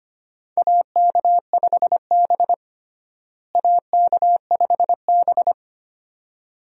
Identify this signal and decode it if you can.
Morse code, heard as a keyed tone, about 25 wpm, 700 Hz. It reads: AK5B AK5B
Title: USA Call Signs - 4 Characters - Repeated 2x - 25wpm